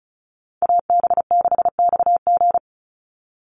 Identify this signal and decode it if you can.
Morse code, heard as a keyed tone, about 34 wpm, 700 Hz. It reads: AB6XG
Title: Call Signs - 35wpm